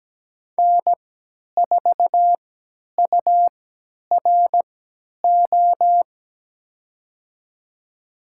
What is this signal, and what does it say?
Morse code, heard as a keyed tone, about 17 wpm, spaced as for 10 wpm, 700 Hz. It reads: N4URO
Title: USA Call Signs - 5 Characters - 3x Character Spacing - 17wpm